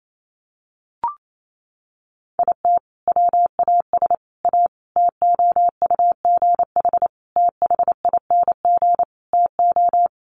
Morse code, heard as a keyed tone, about 28 wpm, 700 Hz. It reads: IT WAS A TOUGH THING TO
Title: Sentences from Top 1200 Words 28wpm